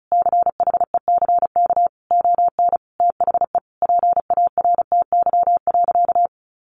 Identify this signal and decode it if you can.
Morse code, heard as a keyed tone, about 34 wpm, 700 Hz. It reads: CHECK ON THE PARTY.
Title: Sentences from Top 1400 Words 35wpm